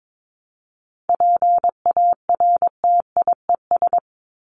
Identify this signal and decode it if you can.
Morse code, heard as a keyed tone, about 22 wpm, 700 Hz. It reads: PARTIES
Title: ES:S Suffix - Top 100 Words with Base Words 22wpm